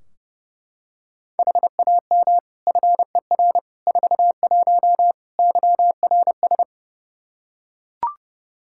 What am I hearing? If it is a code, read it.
Morse code, heard as a keyed tone, about 32 wpm, 700 Hz. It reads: HAM FER 41 YRS
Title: Practice QSO - Exchange 3 - More Info - 30wpm